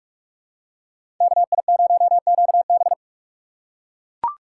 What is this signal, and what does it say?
Morse code, heard as a keyed tone, about 41 wpm, 700 Hz. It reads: KI0QB
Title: USA Call Signs - 5 Characters - 2x Character Spacing - 45wpm